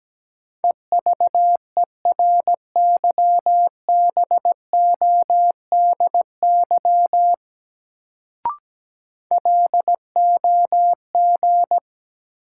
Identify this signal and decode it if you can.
Morse code, heard as a keyed tone, about 17 wpm, 700 Hz. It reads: EVERYBODY LOG
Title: Sentences from Top 1400 Words - New Word Drill - 17wpm